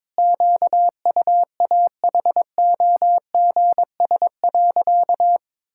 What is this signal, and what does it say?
Morse code, heard as a keyed tone, about 22 wpm, 700 Hz. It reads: QUAHOGS.
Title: Pangram Challenge v2 - 22wpm